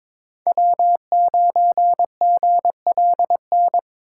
Morse code, heard as a keyed tone, about 22 wpm, 700 Hz. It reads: W9GLN
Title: USA Call Signs - 5 Characters - 3x Character Spacing - 22wpm